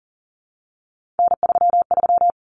Morse code, heard as a keyed tone, about 40 wpm, 700 Hz. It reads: N33
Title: Common Cut Numbers - 3-Digit 40wpm